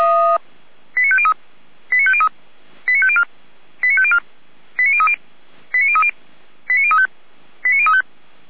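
Five-tone selective call (ZVEI), heard as quick runs of five tones, tones of about 70 ms. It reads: 89503 89503 89504 89504 89030 89030 89035 89035
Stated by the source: ZVEI five-tone selective call received by SDR, feuerwehr-orlamuende.org fire brigade audio archive.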